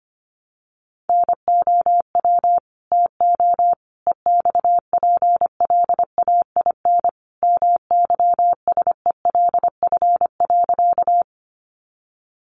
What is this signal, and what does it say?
Morse code, heard as a keyed tone, about 25 wpm, 700 Hz. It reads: NOW TO EXPLAIN MYSELF.
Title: Sentences from Top 1300 Words 25wpm